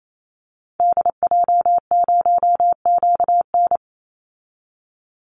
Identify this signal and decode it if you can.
Morse code, heard as a keyed tone, about 28 wpm, 700 Hz. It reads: DJ0QN
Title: Call Signs Repeated 3x - 28wpm